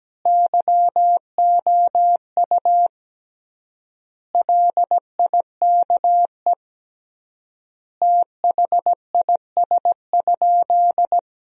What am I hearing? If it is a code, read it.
Morse code, heard as a keyed tone, about 17 wpm, 700 Hz. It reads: YOU LIKE THIS?
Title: 3x Word-Spacing - Sentences from Top 200 Words - 17wpm